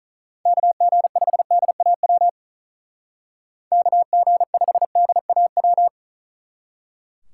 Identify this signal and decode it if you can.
Morse code, heard as a keyed tone, about 44 wpm, 700 Hz. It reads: KG5DAW KG5DAW